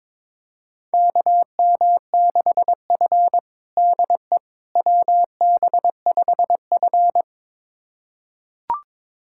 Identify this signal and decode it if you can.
Morse code, heard as a keyed tone, about 22 wpm, 700 Hz. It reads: KM6F DE WB5F
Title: USA Callsign DE Callsign - 4 Characters - 22wpm